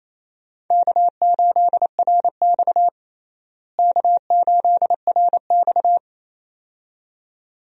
Morse code, heard as a keyed tone, about 27 wpm, 700 Hz. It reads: K8RX K8RX
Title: USA Call Signs - 4 Characters - Repeated 2x - 28wpm